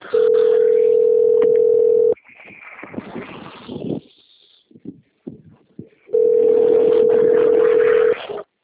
Ringback tone (North America), 440 + 480 Hz.